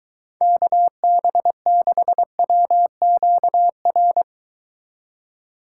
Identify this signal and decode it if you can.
Morse code, heard as a keyed tone, about 22 wpm, 700 Hz. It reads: KB6WQR